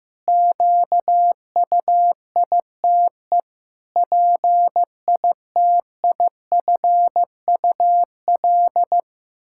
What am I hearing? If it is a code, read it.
Morse code, heard as a keyed tone, about 15 wpm, 700 Hz. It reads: QUITE PITIFUL
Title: Pangram Challenge v2 - 15wpm